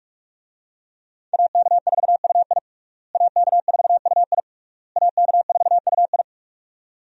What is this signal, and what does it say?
Morse code, heard as a keyed tone, about 46 wpm, 700 Hz. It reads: AK4UI AK4UI AK4UI
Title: USA Call Signs - 5 Characters - Repeated 3x - 45wpm